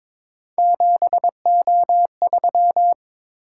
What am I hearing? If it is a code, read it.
Morse code, heard as a keyed tone, about 23 wpm, 700 Hz. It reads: 7O3
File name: practice / 22wpm / Three Letter-Number Combinations - 22wpm